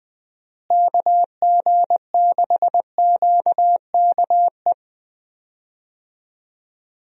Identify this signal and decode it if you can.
Morse code, heard as a keyed tone, about 21 wpm, 700 Hz. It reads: KG6QKE
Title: USA Call Signs - 6 Characters - Repeated 2x - 20wpm